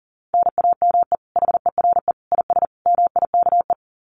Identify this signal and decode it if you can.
Morse code, heard as a keyed tone, about 37 wpm, 700 Hz. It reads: NAME HERE IS MIKE
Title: Common QSO Phrases 40wpm